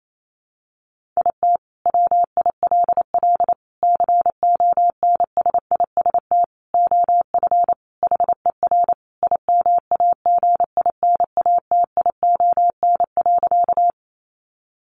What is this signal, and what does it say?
Morse code, heard as a keyed tone, about 28 wpm, 700 Hz. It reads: IT WILL CONSIST OF HER IMAGINATION.
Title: Sentences from Top 1400 Words 28wpm